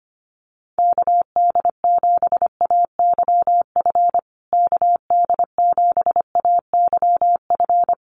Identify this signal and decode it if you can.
Morse code, heard as a keyed tone, about 25 wpm, 700 Hz. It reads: KD7AYF KD7AYF
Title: USA Call Signs Repeated 3x 25wpm